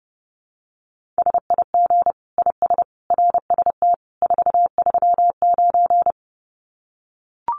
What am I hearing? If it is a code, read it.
Morse code, heard as a keyed tone, about 28 wpm, 700 Hz. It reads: SIG IS RST 439
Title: Common QSO Phrases 30wpm